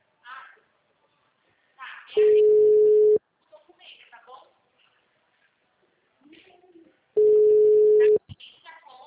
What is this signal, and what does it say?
Ringback tone (Europe, 425 Hz), with speech.
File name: out- -2009- - - .186